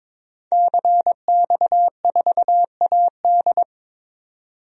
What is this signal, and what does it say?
Morse code, heard as a keyed tone, about 22 wpm, 700 Hz. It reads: CX4AD
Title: DXCC-100 Call Signs - 5 Characters - 22wpm